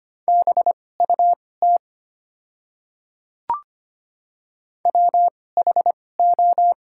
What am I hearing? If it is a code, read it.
Morse code, heard as a keyed tone, about 25 wpm, spaced as for 18 wpm, 700 Hz. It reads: BUT WHO
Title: 2x Character Spacing - Top 300 Words - 25wpm